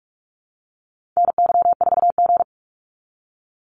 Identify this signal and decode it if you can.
Morse code, heard as a keyed tone, about 46 wpm, 700 Hz. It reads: NY4G
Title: NA SOTA Activator Callsigns 45wpm